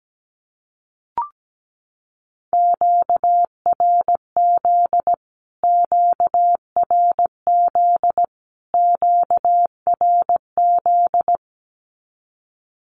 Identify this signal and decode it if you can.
Morse code, heard as a keyed tone, about 17 wpm, 700 Hz. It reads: QRZ QRZ QRZ
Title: Q-Codes Repeated 3x 17wpm